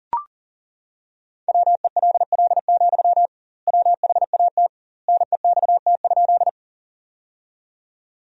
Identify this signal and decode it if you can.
Morse code, heard as a keyed tone, about 37 wpm, 700 Hz. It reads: WELL, WHAT NEXT?
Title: 2x Word-Spacing - Sentences from Top 300 Words - 40wpm